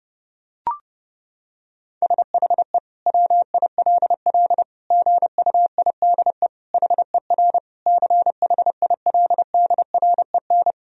Morse code, heard as a keyed tone, about 30 wpm, 700 Hz. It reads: SHE WILL GUIDE HER CHILDREN
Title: Sentences from Top 1000 Words - Review Part 3 - 30wpm